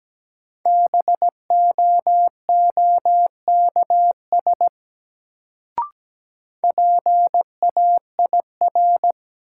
Morse code, heard as a keyed tone, about 17 wpm, 700 Hz. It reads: BOOKS PAIR
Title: Sentences from Top 1200 Words - New Word Drill Alt - 17wpm